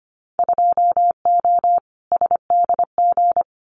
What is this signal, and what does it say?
Morse code, heard as a keyed tone, about 25 wpm, 700 Hz. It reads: 2O SDG
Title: Winter Field Day Exchanges 25wpm